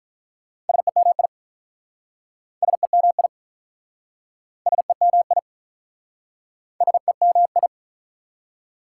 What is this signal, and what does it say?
Morse code, heard as a keyed tone, about 47 wpm, 690 Hz. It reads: SEMI SEMI SEMI SEMI